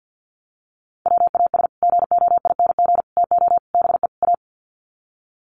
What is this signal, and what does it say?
Morse code, heard as a keyed tone, about 48 wpm, 700 Hz. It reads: WAS GOING TO BE A
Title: Sets of 5 Words 50wpm